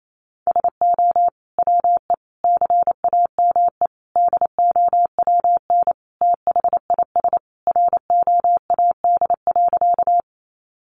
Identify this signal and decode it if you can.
Morse code, heard as a keyed tone, about 28 wpm, 700 Hz. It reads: SO WE CAME DOWN THIS ROAD.
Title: Sentences from Top 400 Words 28wpm